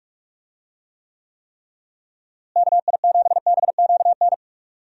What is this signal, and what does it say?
Morse code, heard as a keyed tone, about 46 wpm, 700 Hz. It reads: KI7BQN
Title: USA Call Signs - 6 Characters - Repeated 3x - 45wpm